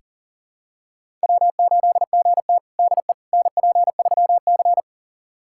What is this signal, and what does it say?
Morse code, heard as a keyed tone, about 40 wpm, 700 Hz. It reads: W8GT DE NP3C
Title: USA Callsign DE Callsign - 4 Characters - 40wpm